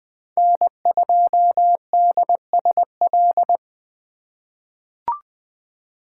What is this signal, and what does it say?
Morse code, heard as a keyed tone, about 21 wpm, 700 Hz. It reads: N2DSL
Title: USA Call Signs - 5 Characters - 3x Character Spacing - 20wpm